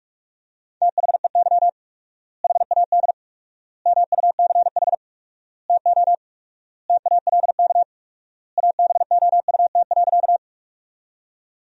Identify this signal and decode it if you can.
Morse code, heard as a keyed tone, about 46 wpm, 700 Hz. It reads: THEY HAD MUCH TO TALK ABOUT.
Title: 4x Word-Spacing - Sentences from Top 400 Words - 45wpm